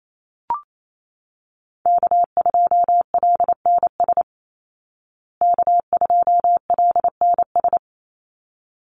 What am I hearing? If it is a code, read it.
Morse code, heard as a keyed tone, about 28 wpm, 700 Hz. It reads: K2LNS K2LNS
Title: Call Signs Repeated 3x - 28wpm